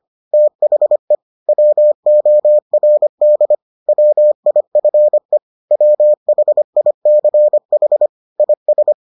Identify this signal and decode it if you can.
Morse code, heard as a keyed tone, about 25 wpm, 600 Hz. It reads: THE WORD WIFE WHICH IS